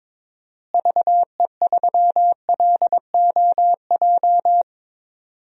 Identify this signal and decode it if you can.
Morse code, heard as a keyed tone, about 22 wpm, 700 Hz. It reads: VE3LOJ
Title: DXCC-100 Call Signs - 6 Characters - Repeated 3x - 22wpm